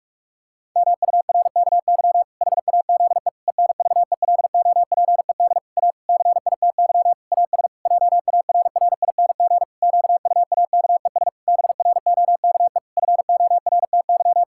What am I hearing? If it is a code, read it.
Morse code, heard as a keyed tone, about 46 wpm, 700 Hz. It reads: MURKY HAZE ENVELOPED A CITY AS JARRING QUAKES BROKE FORTY
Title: Pangram Challenge v2 - 45wpm